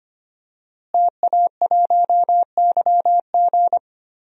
Morse code, heard as a keyed tone, about 25 wpm, 700 Hz. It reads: TA1YG
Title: DXCC-100 Call Signs - 5 Characters - Repeated 3x - 25wpm